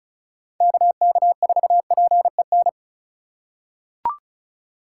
Morse code, heard as a keyed tone, about 36 wpm, 700 Hz. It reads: KK4PEN
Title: USA Call Signs - 8x Character Spacing - 35wpm